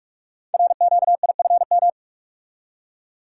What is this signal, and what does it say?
Morse code, heard as a keyed tone, about 44 wpm, 700 Hz. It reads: RQIFM
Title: Code Groups - 5 Letters - 45wpm